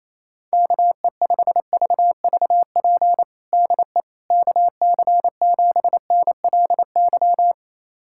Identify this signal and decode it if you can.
Morse code, heard as a keyed tone, about 28 wpm, 700 Hz. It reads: KE5VVP DE KC7NLY
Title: USA Callsign DE Callsign 28wpm